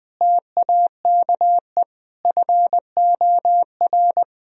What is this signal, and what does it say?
Morse code, heard as a keyed tone, about 20 wpm, 700 Hz. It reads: TAKE FOR
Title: Sentences from Top 1000 Words - Review Part 3 - 20wpm